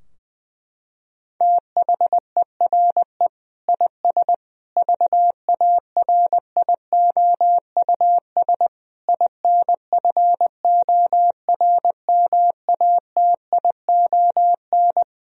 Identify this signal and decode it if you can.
Morse code, heard as a keyed tone, about 20 wpm, 700 Hz. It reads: THERE IS VARIOUS INFORMATION
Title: Sentences from Top 1000 Words - Review Part 4 - 20wpm